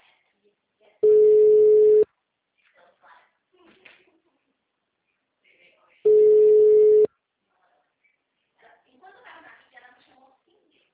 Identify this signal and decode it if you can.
Ringback tone (Europe, 425 Hz), with speech.
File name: out- -2002- - - .116